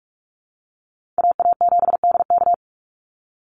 Morse code, heard as a keyed tone, about 44 wpm, 700 Hz. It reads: AA7DK
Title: NA SOTA Chaser Callsigns 45wpm